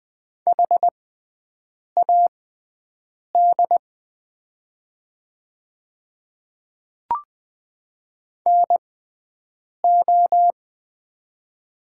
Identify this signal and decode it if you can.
Morse code, heard as a keyed tone, about 20 wpm, spaced as for 7 wpm, 700 Hz. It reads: HAD NO
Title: 6x Character Spacing - Top 200 Words - 20wpm